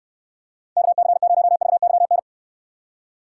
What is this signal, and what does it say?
Morse code, heard as a keyed tone, about 66 wpm, 700 Hz. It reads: KC0VYN